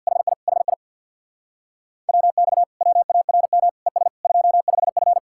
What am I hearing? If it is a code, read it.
Morse code, heard as a keyed tone, about 64 wpm, 700 Hz. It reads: HI HI WX WARM ES 25F